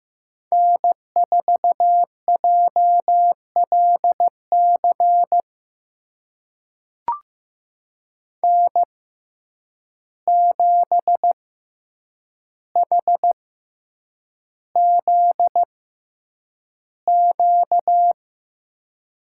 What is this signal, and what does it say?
Morse code, heard as a keyed tone, about 15 wpm, spaced as for 8 wpm, 700 Hz. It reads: N4JLC N7HZQ
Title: USA Call Signs - 5 Characters - 6x Character Spacing - 15wpm